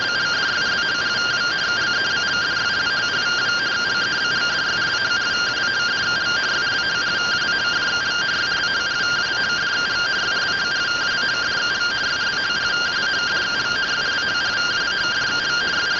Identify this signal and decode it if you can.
Two-tone teleprinter signal, about 170 Hz shift.
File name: findings / AXIAL fr130 3mm 409.gen-36 rtty rx